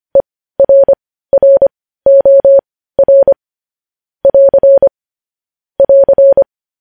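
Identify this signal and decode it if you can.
Morse code, heard as a keyed tone, about 25 wpm, spaced as for 15 wpm, 550 Hz. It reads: ERROR <AR> <AR>
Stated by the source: Morse code news headlines at 15 wpm.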